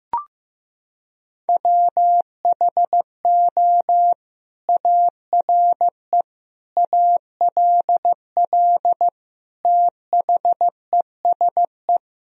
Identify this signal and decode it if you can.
Morse code, heard as a keyed tone, about 15 wpm, 700 Hz. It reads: WHO ARE ALL THESE
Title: Sentences from Top 1000 Words - Review Part 1 - 15wpm